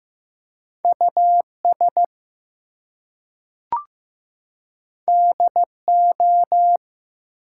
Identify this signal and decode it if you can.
Morse code, heard as a keyed tone, about 15 wpm, 700 Hz. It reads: US DO
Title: Two Letter Words - 15wpm